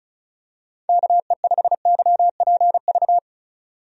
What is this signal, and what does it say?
Morse code, heard as a keyed tone, about 36 wpm, 700 Hz. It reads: KE5YPV
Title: USA Call Signs - 6 Characters - 35wpm